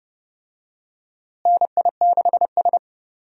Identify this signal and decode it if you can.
Morse code, heard as a keyed tone, about 30 wpm, 700 Hz. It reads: NI6S
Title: USA Call Signs - 4 Characters - Repeated 3x - 30wpm